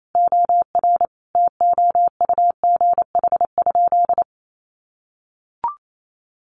Morse code, heard as a keyed tone, about 28 wpm, 700 Hz. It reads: OR TOUGH?
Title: Sentences from Top 1200 Words 28wpm